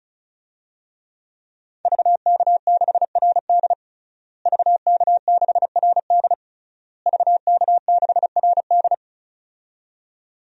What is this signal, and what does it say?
Morse code, heard as a keyed tone, about 36 wpm, 700 Hz. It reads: VK6RD VK6RD VK6RD
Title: DXCC-100 Call Signs - 5 Characters - Repeated 3x - 35wpm